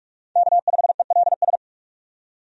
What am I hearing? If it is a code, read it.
Morse code, heard as a keyed tone, about 46 wpm, 700 Hz. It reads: K5ELS
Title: USA Call Signs - 5 Characters - 45wpm